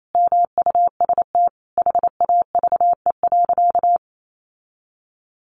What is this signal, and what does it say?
Morse code, heard as a keyed tone, about 28 wpm, 700 Hz. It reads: MUST HAVE.
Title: Sentences from Top 1100 Words 28wpm